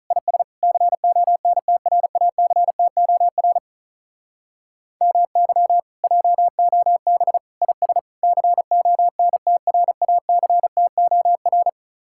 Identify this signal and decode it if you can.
Morse code, heard as a keyed tone, about 38 wpm, 700 Hz. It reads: IS CONTRACTOR MY JOB IS CONTRACTOR
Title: Speed-Racing Common QSO Phrases - 35wpm